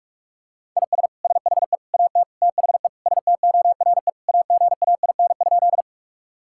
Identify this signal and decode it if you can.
Morse code, heard as a keyed tone, about 46 wpm, 700 Hz. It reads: IS SHE AT THE STORE AGAIN?
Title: Sentences from Top 1000 Words - Review Part 5 - 45wpm